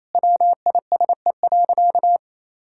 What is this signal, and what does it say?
Morse code, heard as a keyed tone, about 28 wpm, 700 Hz. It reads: WISE.
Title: Sentences from Top 1600 Words 28wpm